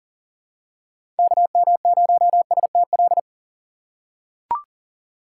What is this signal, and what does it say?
Morse code, heard as a keyed tone, about 38 wpm, 700 Hz. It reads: KM0STL
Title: USA Call Signs - 6 Characters - 6x Character Spacing - 40wpm